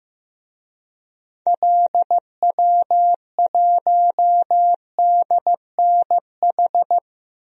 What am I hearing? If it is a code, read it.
Morse code, heard as a keyed tone, about 15 wpm, 700 Hz. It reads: LW1DNH
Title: DXCC-100 Call Signs - 6 Characters - Repeated 2x - 15wpm